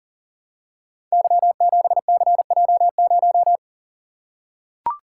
Morse code, heard as a keyed tone, about 39 wpm, 700 Hz. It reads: Y7CJ0
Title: Code Groups - 5 Letter:Numbers - 40wpm